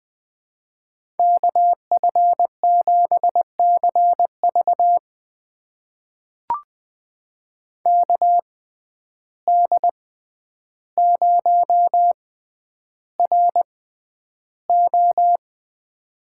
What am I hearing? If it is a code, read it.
Morse code, heard as a keyed tone, about 20 wpm, spaced as for 10 wpm, 700 Hz. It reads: KF7CV KD0RO
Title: USA Call Signs - 5 Characters - 6x Character Spacing - 20wpm